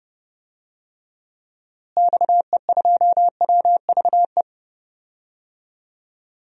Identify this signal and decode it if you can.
Morse code, heard as a keyed tone, about 30 wpm, 700 Hz. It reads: XE2WVE
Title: DXCC-100 Call Signs - 6 Characters - Repeated 2x - 30wpm